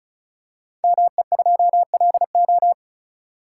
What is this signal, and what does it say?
Morse code, heard as a keyed tone, about 36 wpm, 700 Hz. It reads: ME2LO
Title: Code Groups - 5 Letter:Numbers - 35wpm